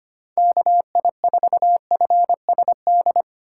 Morse code, heard as a keyed tone, about 25 wpm, 700 Hz. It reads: KI4FSD
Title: USA Call Signs - 6 Characters - 6x Character Spacing - 25wpm